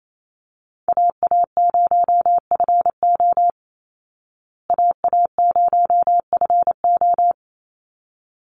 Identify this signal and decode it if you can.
Morse code, heard as a keyed tone, about 28 wpm, 700 Hz. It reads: AA0FO AA0FO
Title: Call Signs Repeated 3x - 28wpm